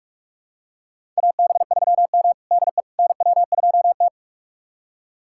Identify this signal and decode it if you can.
Morse code, heard as a keyed tone, about 46 wpm, 700 Hz. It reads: AB3M DE NW2T
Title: USA Callsign DE Callsign - 4 Characters - 45wpm